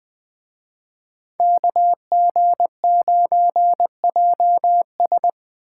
Morse code, heard as a keyed tone, about 21 wpm, 700 Hz. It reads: KG9JS